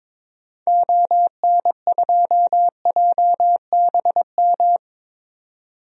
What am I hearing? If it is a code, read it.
Morse code, heard as a keyed tone, about 22 wpm, 700 Hz. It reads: ON2JBM
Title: DXCC-100 Call Signs - 6 Characters - 22wpm